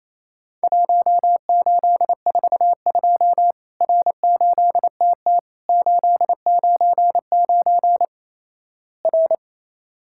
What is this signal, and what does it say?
Morse code, heard as a keyed tone, about 28 wpm, 700 Hz. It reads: 1842 R8TT 899 R
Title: Worked All Europe DX Contest - QTC Exchange - 28wpm